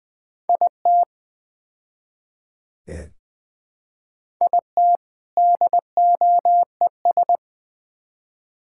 Morse code, heard as a keyed tone, about 21 wpm, 700 Hz. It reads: IT IT DOES
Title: Build A Phrase - 4-Word Phrases - 20wpm